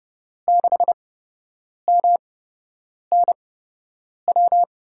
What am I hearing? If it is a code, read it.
Morse code, heard as a keyed tone, about 30 wpm, spaced as for 8 wpm, 700 Hz. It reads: 6MNW
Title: USA Call Signs - 6 Characters - 8x Character Spacing - 30wpm